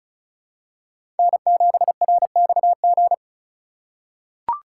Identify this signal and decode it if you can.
Morse code, heard as a keyed tone, about 35 wpm, 700 Hz. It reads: N7RXG
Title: USA Call Signs - 5 Characters - 2x Character Spacing - 35wpm